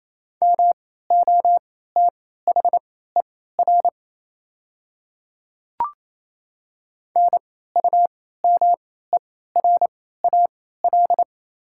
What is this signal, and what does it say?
Morse code, heard as a keyed tone, about 27 wpm, spaced as for 16 wpm, 690 Hz. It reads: MOTHER NUMERAL
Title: 3x Character Spacing - Top 500 Words - 28wpm